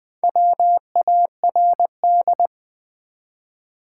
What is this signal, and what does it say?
Morse code, heard as a keyed tone, about 20 wpm, 700 Hz. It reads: WARD
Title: Build A Word - 2 Letters At A Time - 4-Letter Words - 20wpm